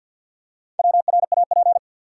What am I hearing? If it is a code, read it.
Morse code, heard as a keyed tone, about 52 wpm, 700 Hz. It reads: WRAP
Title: Four Letter Words - 50wpm